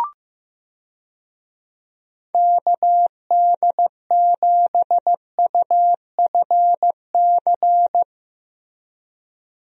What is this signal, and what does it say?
Morse code, heard as a keyed tone, about 16 wpm, 700 Hz. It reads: KD7UFC
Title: USA Call Signs - 6 Characters - Repeated 3x - 15wpm